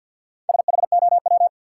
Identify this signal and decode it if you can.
Morse code, heard as a keyed tone, about 52 wpm, 700 Hz. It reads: SHOW
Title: Sets of 5 Words - New Word Drill - 50wpm